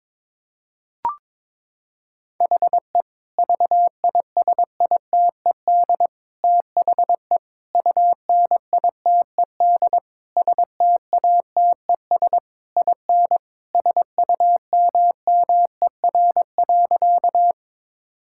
Morse code, heard as a keyed tone, about 22 wpm, 700 Hz. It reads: HE VISITED THE UNITED STATES IN SUMMER.
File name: practice / 22wpm / Sentences with 4-Word Phrases - 22wpm